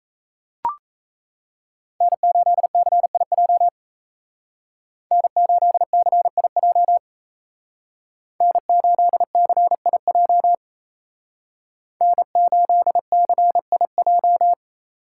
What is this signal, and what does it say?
Morse code, heard as a keyed tone, about 36 wpm, 700 Hz. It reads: N8CIJ N8CIJ N8CIJ N8CIJ